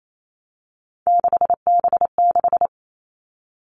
Morse code, heard as a keyed tone, about 28 wpm, 700 Hz. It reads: 6B6
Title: All Cut Numbers - 3-Digit 28wpm